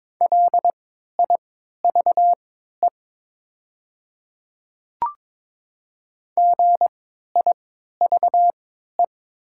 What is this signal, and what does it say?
Morse code, heard as a keyed tone, about 22 wpm, spaced as for 12 wpm, 700 Hz. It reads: LIVE GIVE
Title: 3x Character Spacing - Top 200 Words - 22wpm